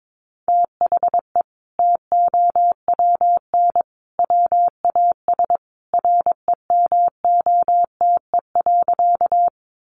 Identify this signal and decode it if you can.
Morse code, heard as a keyed tone, about 22 wpm, 700 Hz. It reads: THE TOWN WAS REMOTE.
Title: Sentences from Top 1300 Words 22wpm